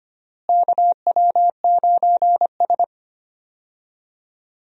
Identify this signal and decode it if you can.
Morse code, heard as a keyed tone, about 25 wpm, 700 Hz. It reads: KW9S
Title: USA Call Signs - 4 Characters - Repeated 2x - 25wpm